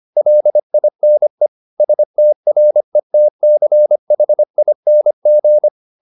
Morse code, heard as a keyed tone, about 25 wpm, 600 Hz. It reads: LINE STRETCHING
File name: the old man and the sea part 1 25wpm